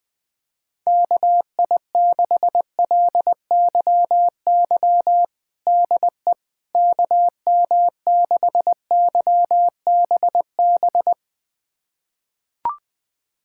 Morse code, heard as a keyed tone, about 20 wpm, 700 Hz. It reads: KI6LYY DE KM6YBB
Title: USA Callsign DE Callsign - 6 Characters - 20wpm